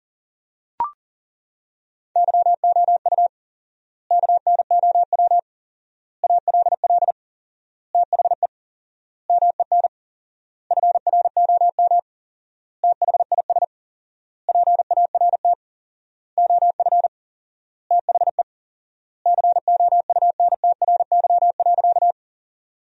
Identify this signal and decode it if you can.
Morse code, heard as a keyed tone, about 40 wpm, 700 Hz. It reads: YOU KNOW ALL THE MEN FROM THIS PART OF THE COUNTRY.
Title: 4x Word-Spacing - Sentences from Top 300 Words - 40wpm